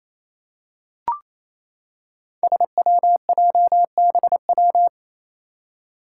Morse code, heard as a keyed tone, about 28 wpm, 700 Hz. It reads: SWJBW
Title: Five Challenging Letter-Number Combinations 28wpm